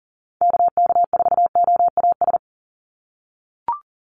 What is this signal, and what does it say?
Morse code, heard as a keyed tone, about 40 wpm, 700 Hz. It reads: KK4OAS
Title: USA Call Signs 40wpm